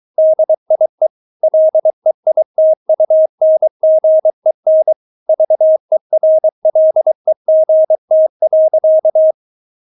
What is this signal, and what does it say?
Morse code, heard as a keyed tone, about 23 wpm, 630 Hz. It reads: DIE LEITUNGEN VERLEGT.
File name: Witz4 23wpm